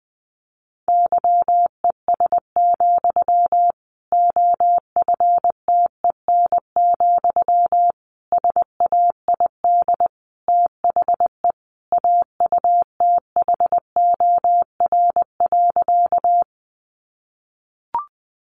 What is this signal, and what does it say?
Morse code, heard as a keyed tone, about 19 wpm, 700 Hz. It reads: YES, OFTEN, SAID THE AUTHOR.
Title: Sentences from Top 1200 Words 20wpm